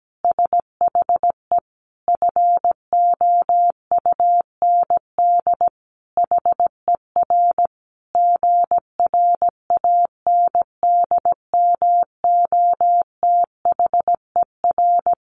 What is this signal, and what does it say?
Morse code, heard as a keyed tone, about 17 wpm, 700 Hz. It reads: SHE FOUND HER GRANDMOTHER
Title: Sentences from Top 1200 Words 17wpm